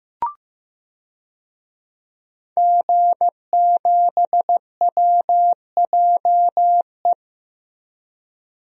Morse code, heard as a keyed tone, about 15 wpm, 700 Hz. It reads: G7WJE
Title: DXCC-100 Call Signs - 5 Characters - 15wpm